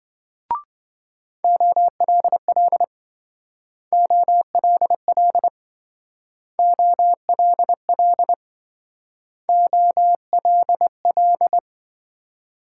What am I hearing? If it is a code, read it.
Morse code, heard as a keyed tone, about 25 wpm, 700 Hz. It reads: OLL OLL OLL OLL
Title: Speed-Racing Multi-letter Phonemes - 20wpm